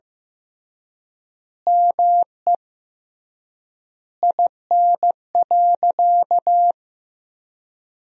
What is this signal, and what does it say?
Morse code, heard as a keyed tone, about 15 wpm, 700 Hz. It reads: ME IN.
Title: 3x Word-Spacing - Sentences from Top 500 Words - 15wpm